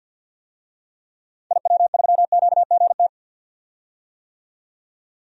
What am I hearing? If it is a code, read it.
Morse code, heard as a keyed tone, about 48 wpm, 700 Hz. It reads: IW3QGT
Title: DXCC-100 Call Signs - 6 Characters - 50wpm